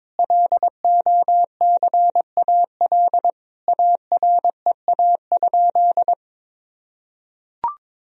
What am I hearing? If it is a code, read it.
Morse code, heard as a keyed tone, about 22 wpm, 700 Hz. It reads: LOCAL AREA?
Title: Sentences from Top 1000 Words - Review Part 3 - 22wpm